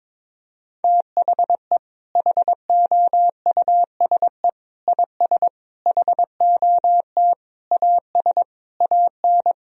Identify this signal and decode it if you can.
Morse code, heard as a keyed tone, about 22 wpm, 700 Hz. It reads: THE HOUSE IS HOT AS AN
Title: Sentences from Top 1000 Words - Review Part 4 - 22wpm